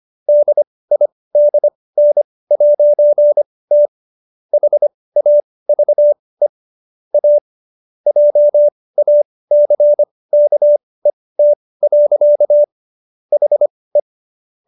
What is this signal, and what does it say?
Morse code, heard as a keyed tone, about 25 wpm, spaced as for 18 wpm, 590 Hz. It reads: DIDN'T HAVE A JACKET. HE